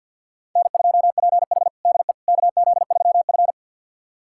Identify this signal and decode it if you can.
Morse code, heard as a keyed tone, about 51 wpm, 700 Hz. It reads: N2PH DE KZ3F
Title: USA Callsign DE Callsign - 4 Characters - 50wpm